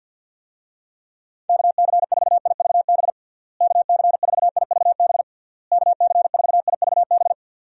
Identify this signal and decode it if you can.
Morse code, heard as a keyed tone, about 46 wpm, 700 Hz. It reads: KC4IVB KC4IVB KC4IVB
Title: USA Call Signs - 6 Characters - Repeated 3x - 50wpm